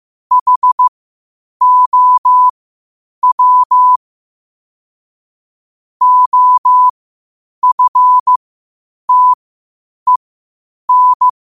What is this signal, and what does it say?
Morse code, heard as a keyed tone, about 15 wpm, spaced as for 8 wpm, 1000 Hz. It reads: HOW OFTEN
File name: qotdfiles / 8wpm